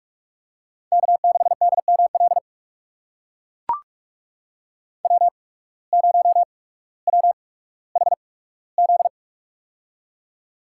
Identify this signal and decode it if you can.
Morse code, heard as a keyed tone, about 43 wpm, spaced as for 19 wpm, 700 Hz. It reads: K6DML W0WHZ
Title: USA Call Signs - 8x Character Spacing - 45wpm